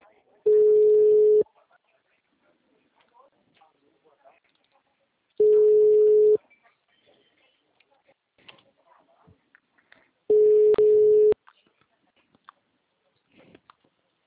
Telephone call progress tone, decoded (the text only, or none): ringback tone